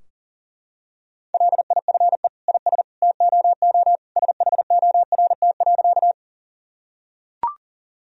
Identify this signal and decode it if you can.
Morse code, heard as a keyed tone, about 40 wpm, 700 Hz. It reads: LIFE IS TOO SHORT.
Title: Sentences from Top 1000 Words - Review Part 2 - 40wpm